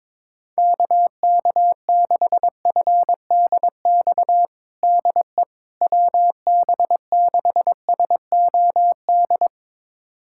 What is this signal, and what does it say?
Morse code, heard as a keyed tone, about 22 wpm, 700 Hz. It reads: KK6FDX DE WB6SOD
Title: USA Callsign DE Callsign - 6 Characters - 22wpm